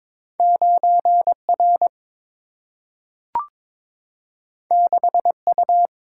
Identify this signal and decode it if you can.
Morse code, heard as a keyed tone, about 22 wpm, 700 Hz. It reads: 9 R 6 U
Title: Single-Letter Single-Number 22wpm